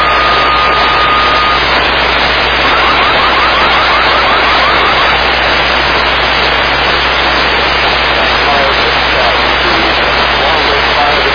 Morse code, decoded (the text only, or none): none